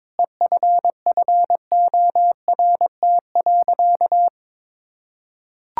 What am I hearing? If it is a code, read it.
Morse code, heard as a keyed tone, about 22 wpm, 700 Hz. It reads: EFFORT.
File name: practice / 22wpm / Sentences with 8-Word Phrases - 22wpm